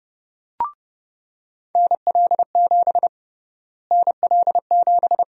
Morse code, heard as a keyed tone, about 32 wpm, 700 Hz. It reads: NL7 NL7
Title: USA Call Sign Prefixes - Repeated 2x - 30wpm